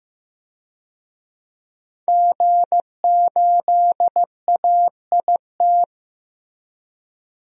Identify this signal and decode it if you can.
Morse code, heard as a keyed tone, about 15 wpm, 700 Hz. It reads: G8AIT
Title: DXCC-100 Call Signs - 5 Characters - Repeated 3x - 15wpm